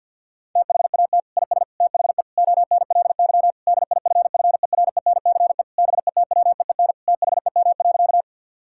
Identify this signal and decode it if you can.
Morse code, heard as a keyed tone, about 50 wpm, 700 Hz. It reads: THAT IS THE ONLY DIFFERENCE BETWEEN THEM.
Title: Sentences from Top 1000 Words - Review Part 5 - 50wpm